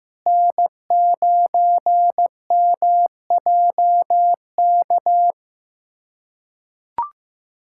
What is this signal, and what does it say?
Morse code, heard as a keyed tone, about 16 wpm, 700 Hz. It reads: N9MJK
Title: USA Callsign DE Callsign - 5 Characters - 15wpm